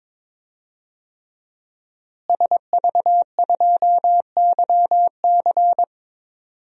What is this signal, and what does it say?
Morse code, heard as a keyed tone, about 22 wpm, 700 Hz. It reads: SV2YC
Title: DXCC-100 Call Signs - 5 Characters - Repeated 3x - 22wpm